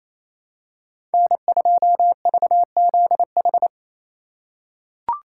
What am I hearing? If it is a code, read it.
Morse code, heard as a keyed tone, about 28 wpm, 700 Hz. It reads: N2VZH
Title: USA Call Signs - 5 Characters - 4x Character Spacing - 28wpm